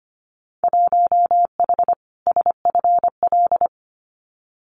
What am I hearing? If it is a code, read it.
Morse code, heard as a keyed tone, about 25 wpm, 700 Hz. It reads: 1H SFL
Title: Winter Field Day Exchanges 25wpm